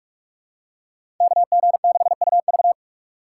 Morse code, heard as a keyed tone, about 46 wpm, 700 Hz. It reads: KG6UV
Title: USA Call Signs - 5 Characters - 4x Character Spacing - 45wpm